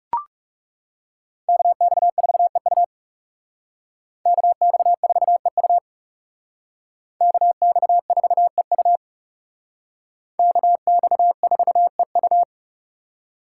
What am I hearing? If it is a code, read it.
Morse code, heard as a keyed tone, about 38 wpm, 700 Hz. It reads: KX4EU KX4EU KX4EU KX4EU